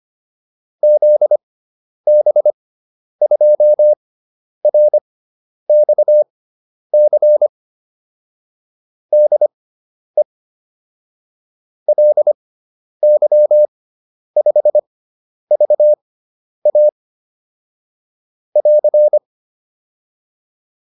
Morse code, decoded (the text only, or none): ZB2RXC DE LY5VA <AR>